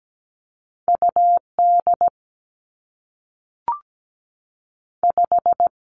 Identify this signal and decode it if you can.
Morse code, heard as a keyed tone, about 17 wpm, 700 Hz. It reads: U D 5
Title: Two Challenging Letter-Number Combinations 17wpm